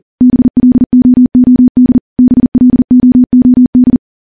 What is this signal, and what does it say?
Morse code, heard as a keyed tone, about 43 wpm, 250 Hz. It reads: BLOOD BLOOD
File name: morse-training-40WPM